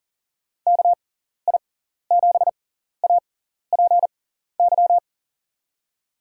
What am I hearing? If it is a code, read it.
Morse code, heard as a keyed tone, about 40 wpm, spaced as for 14 wpm, 700 Hz. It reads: KI7APY
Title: USA Call Signs - 6 Characters - 6x Character Spacing - 40wpm